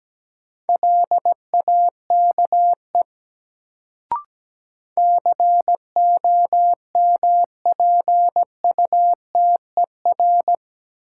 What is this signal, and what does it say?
Morse code, heard as a keyed tone, about 17 wpm, 700 Hz. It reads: LAKE COMPUTER
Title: Sentences from Top 1000 Words - Review Part 3 - New Word Drill - 17wpm